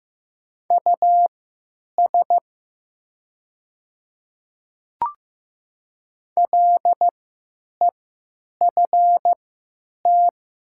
Morse code, heard as a keyed tone, about 15 wpm, spaced as for 9 wpm, 700 Hz. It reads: US LEFT
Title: 3x Character Spacing - Top 500 Words - 15wpm